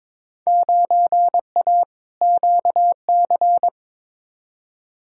Morse code, heard as a keyed tone, about 21 wpm, 700 Hz. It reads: 9A QC
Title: Field Day Exchanges Encore - 22wpm